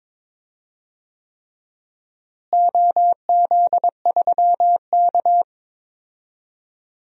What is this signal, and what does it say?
Morse code, heard as a keyed tone, about 22 wpm, 700 Hz. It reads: OZ3K
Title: DXCC-100 Call Signs - 4 Characters - Repeated 3x - 22wpm